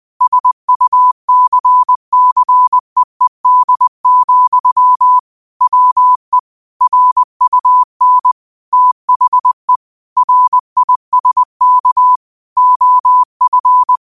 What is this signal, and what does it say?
Morse code, heard as a keyed tone, about 20 wpm, 1000 Hz. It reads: SUCCEED, WE RUN THE RISK OF
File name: qotdfiles / 20wpm